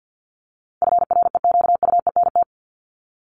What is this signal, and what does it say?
Morse code, heard as a keyed tone, about 50 wpm, 700 Hz. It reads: FREQUENT
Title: LY Suffix - Top 100 Words with Base Words 50wpm